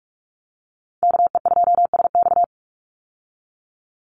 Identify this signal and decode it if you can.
Morse code, heard as a keyed tone, about 44 wpm, 700 Hz. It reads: KE2SX
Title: Call Signs Repeated 2x - 45wpm